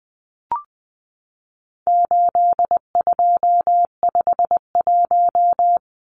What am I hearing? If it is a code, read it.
Morse code, heard as a keyed tone, about 20 wpm, 700 Hz. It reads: 8251
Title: Numbers - 4-Digit 20wpm